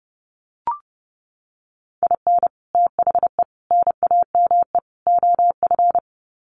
Morse code, heard as a keyed tone, about 30 wpm, 700 Hz. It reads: IN THE NAME OF
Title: Sets of 4 Words 30wpm